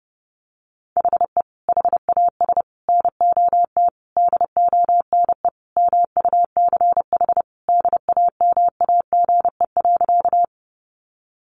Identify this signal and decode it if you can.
Morse code, heard as a keyed tone, about 30 wpm, 700 Hz. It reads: HE HAS NOT DONE MUCH DAMAGE.
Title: Sentences from Top 1100 Words 30wpm